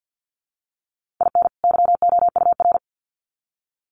Morse code, heard as a keyed tone, about 50 wpm, 700 Hz. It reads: IN YOUR
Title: Sets of 2 Words 50wpm